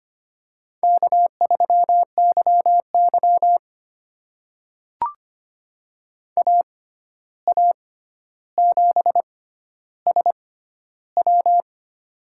Morse code, heard as a keyed tone, about 25 wpm, spaced as for 12 wpm, 700 Hz. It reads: K3YY AA7SW
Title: USA Call Signs - 6x Character Spacing - 25wpm